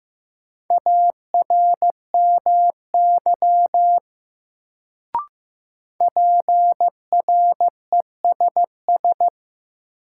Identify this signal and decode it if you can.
Morse code, heard as a keyed tone, about 16 wpm, 700 Hz. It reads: ARMY PRESS
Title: Sentences from Top 800 Words - New Word Drill - 15wpm